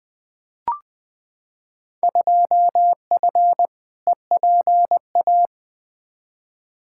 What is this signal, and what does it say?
Morse code, heard as a keyed tone, about 20 wpm, 700 Hz. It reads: 2F EPA
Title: Field Day Exchanges - 20wpm